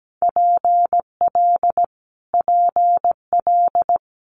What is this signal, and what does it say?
Morse code, heard as a keyed tone, about 17 wpm, 700 Hz. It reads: PL PL
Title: Multi-letter Phonemes Repeated 3x 17wpm